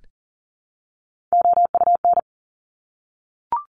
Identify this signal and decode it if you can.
Morse code, heard as a keyed tone, about 40 wpm, 700 Hz. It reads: OUN
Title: Top Three Letter Combinations in Words 40wpm